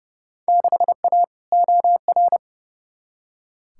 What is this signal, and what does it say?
Morse code, heard as a keyed tone, about 30 wpm, 700 Hz. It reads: 6A OR
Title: Field Day Exchanges - 30wpm